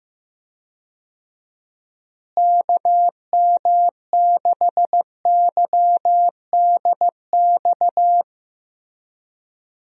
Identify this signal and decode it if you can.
Morse code, heard as a keyed tone, about 15 wpm, 700 Hz. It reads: KM6YDX
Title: USA Call Signs - 6 Characters - Repeated 3x - 15wpm